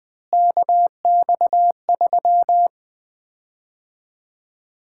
Morse code, heard as a keyed tone, about 20 wpm, 700 Hz. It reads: KX3
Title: Common QSO Phrases - 4x Word-Spacing - 20wpm